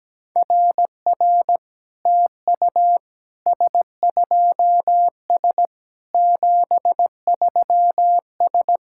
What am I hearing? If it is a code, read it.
Morse code, heard as a keyed tone, about 17 wpm, 700 Hz. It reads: RR TU S2S 73S
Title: SOTA QSO Practice 17wpm